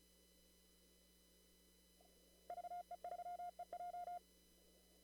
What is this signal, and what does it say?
Morse code, heard as a keyed tone, about 34 wpm, 710 Hz. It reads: VE3EJ
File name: CW / kt5j wpx cw-op-k5pi